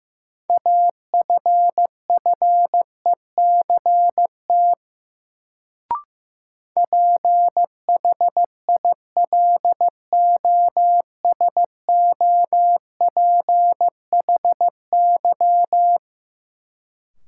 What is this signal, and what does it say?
Morse code, heard as a keyed tone, about 15 wpm, 700 Hz. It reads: AFFECT PHILOSOPHY
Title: Sentences from Top 900 Words - New Word Drill - 15wpm